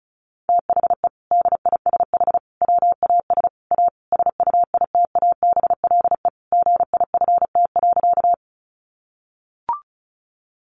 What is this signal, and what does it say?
Morse code, heard as a keyed tone, about 34 wpm, 700 Hz. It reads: THE DISH WAS A SUITABLE GIFT.
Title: Sentences from Top 1400 Words 35wpm